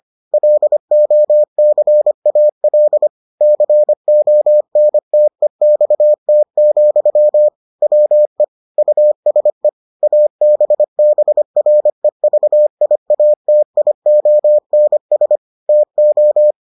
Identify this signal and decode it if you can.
Morse code, heard as a keyed tone, about 25 wpm, 600 Hz. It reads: LOCAL CONTEXT, WE USE ABBREVIATIONS TO